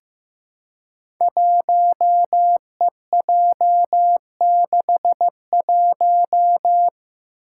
Morse code, heard as a keyed tone, about 15 wpm, 700 Hz. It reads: 1EJ61
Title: Five Challenging Letter-Number Combinations 15wpm